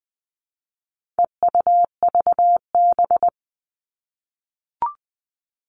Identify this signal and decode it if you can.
Morse code, heard as a keyed tone, about 20 wpm, 700 Hz. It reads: EUVB
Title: All Cut Numbers - 4-Digit 20wpm